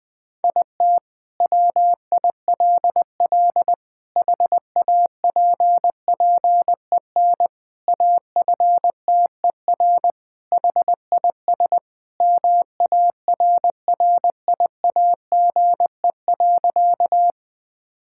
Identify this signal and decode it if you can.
Morse code, heard as a keyed tone, about 20 wpm, 700 Hz. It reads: IT WILL HAPPEN AFTER HIS MARRIAGE.
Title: Sentences from Top 1000 Words - Review Part 5 - 20wpm